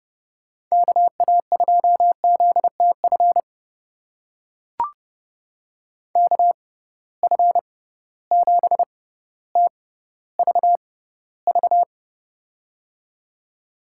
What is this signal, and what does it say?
Morse code, heard as a keyed tone, about 30 wpm, spaced as for 15 wpm, 700 Hz. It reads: KA2ZTF KF7TVV
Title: USA Call Signs - 6 Characters - 6x Character Spacing - 30wpm